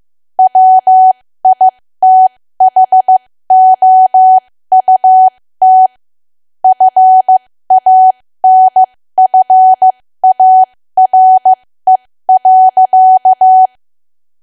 Morse code, heard as a keyed tone, about 15 wpm, spaced as for 13 wpm, 750 Hz. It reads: WITHOUT FANFARE.
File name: cwa / W1AW-Level2 / 13WPM